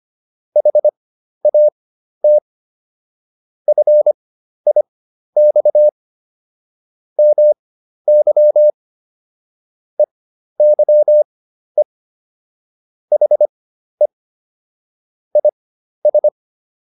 Morse code, heard as a keyed tone, about 25 wpm, spaced as for 12 wpm, 600 Hz. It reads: HAT FIX MY EYE HE IS